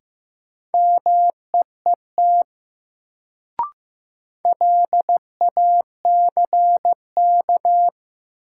Morse code, heard as a keyed tone, about 15 wpm, 700 Hz. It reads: MEET LACK
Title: Sentences from Top 1100 Words - New Word Drill Alt - 15wpm